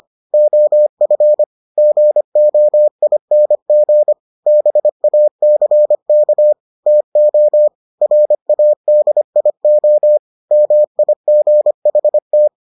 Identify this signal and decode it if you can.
Morse code, heard as a keyed tone, about 25 wpm, 600 Hz. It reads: OF GOING BACK TO RADIO MIGHT